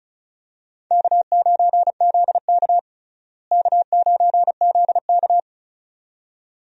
Morse code, heard as a keyed tone, about 36 wpm, 700 Hz. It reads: K9ZK K9ZK
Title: USA Call Signs - 4 Characters - Repeated 2x - 35wpm